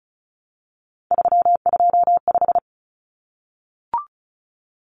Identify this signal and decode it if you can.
Morse code, heard as a keyed tone, about 34 wpm, 700 Hz. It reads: 325
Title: Numbers - 3-Digit 35wpm